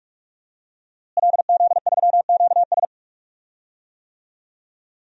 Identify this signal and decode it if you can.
Morse code, heard as a keyed tone, about 47 wpm, 700 Hz. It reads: LZ3QS
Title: DXCC-100 Call Signs - 5 Characters - Repeated 3x - 45wpm